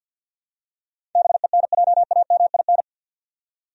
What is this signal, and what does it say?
Morse code, heard as a keyed tone, about 52 wpm, 700 Hz. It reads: BENJAMIN
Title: USA Common Names - 50wpm